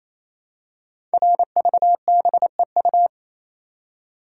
Morse code, heard as a keyed tone, about 27 wpm, 700 Hz. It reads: RVBEU
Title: Code Groups - 5 Letters - 28wpm